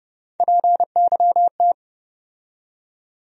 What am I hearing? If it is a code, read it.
Morse code, heard as a keyed tone, about 29 wpm, 700 Hz. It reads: PYT
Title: USA Call Sign Postfixes - 3 Letters 30wpm